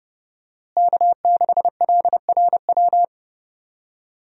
Morse code, heard as a keyed tone, about 30 wpm, 700 Hz. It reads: K6LRW
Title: USA Call Signs - 5 Characters - Repeated 2x - 30wpm